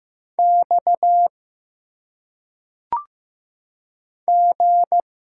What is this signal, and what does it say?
Morse code, heard as a keyed tone, about 15 wpm, 700 Hz. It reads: X G
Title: Single Letters - Phonetic Pronunciation 15wpm